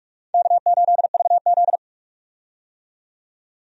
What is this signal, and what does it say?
Morse code, heard as a keyed tone, about 46 wpm, 700 Hz. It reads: K8VZ
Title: USA Call Signs - 4 Characters - 45wpm